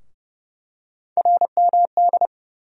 Morse code, heard as a keyed tone, about 30 wpm, 700 Hz. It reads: RMD
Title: USA Call Sign Postfixes - 3 Letters 30wpm